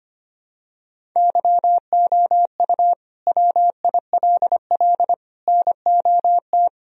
Morse code, heard as a keyed tone, about 25 wpm, 700 Hz. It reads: YOU WILL NOT
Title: Sentences from Top 1000 Words - Review Part 1 - 25wpm